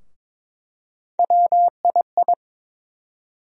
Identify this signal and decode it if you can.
Morse code, heard as a keyed tone, about 21 wpm, 700 Hz. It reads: WII
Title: USA Call Sign Postfixes - 3 Letters 22wpm